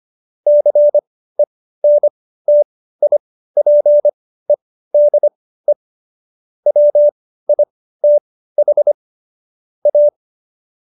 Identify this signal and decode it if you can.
Morse code, heard as a keyed tone, about 25 wpm, spaced as for 15 wpm, 590 Hz. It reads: CENTIPEDE WITH A
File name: Jokes-25-char-15-eff